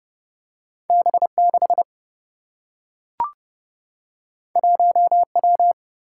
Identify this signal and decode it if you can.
Morse code, heard as a keyed tone, about 30 wpm, 700 Hz. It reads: B6 1W
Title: Single-Letter Single-Number 30wpm